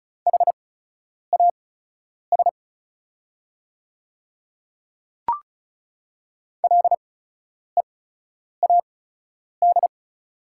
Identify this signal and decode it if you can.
Morse code, heard as a keyed tone, about 34 wpm, spaced as for 10 wpm, 690 Hz. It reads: HAS LEAD
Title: 8x Character Spacing - Top 500 Words - 35wpm